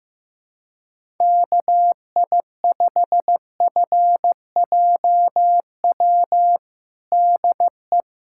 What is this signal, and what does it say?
Morse code, heard as a keyed tone, about 15 wpm, 700 Hz. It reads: KI5FJW DE
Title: USA Callsign DE Callsign - 6 Characters - 15wpm